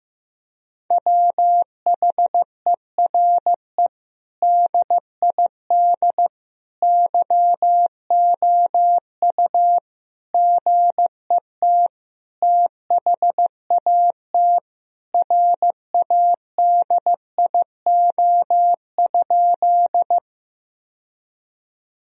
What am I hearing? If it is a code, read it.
Morse code, heard as a keyed tone, about 15 wpm, 700 Hz. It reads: WHERE DID YOU GET THAT RADIO?
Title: Sentences from Top 1000 Words - Review Part 3 - 15wpm